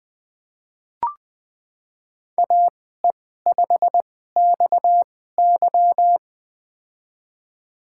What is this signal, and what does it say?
Morse code, heard as a keyed tone, about 20 wpm, spaced as for 14 wpm, 700 Hz. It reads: AE5XY
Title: USA Call Signs - 5 Characters - 2x Character Spacing - 20wpm